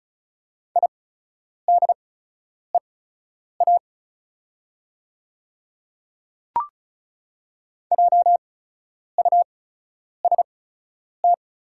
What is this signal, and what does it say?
Morse code, heard as a keyed tone, about 36 wpm, spaced as for 10 wpm, 700 Hz. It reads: IDEA JUST
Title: 8x Character Spacing - Top 500 Words - 35wpm